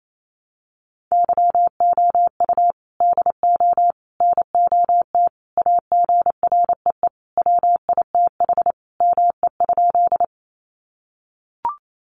Morse code, heard as a keyed tone, about 28 wpm, 700 Hz. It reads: YOU DO NOT AGREE WITH ME?
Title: Sentences from Top 1100 Words 28wpm